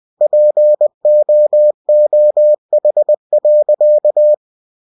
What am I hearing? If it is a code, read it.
Morse code, heard as a keyed tone, about 20 wpm, 600 Hz. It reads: POOH.
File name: cw-books / winnie-the-pooh / winnie-20wpm-page-0014